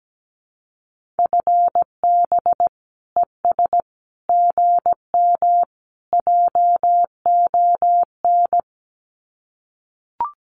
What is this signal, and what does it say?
Morse code, heard as a keyed tone, about 17 wpm, 700 Hz. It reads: FB ES GM JON
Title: Common QSO Phrases Encore 17wpm